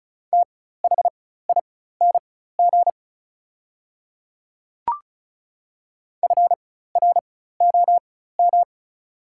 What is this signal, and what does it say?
Morse code, heard as a keyed tone, about 36 wpm, spaced as for 16 wpm, 700 Hz. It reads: THING FROM
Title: 4x Character Spacing - Top 100 Words - 35wpm